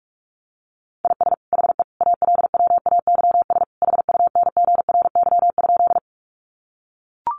Morse code, heard as a keyed tone, about 44 wpm, 700 Hz. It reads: IS HE ALWAYS HUNGRY?
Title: Sentences from Top 1400 Words 45wpm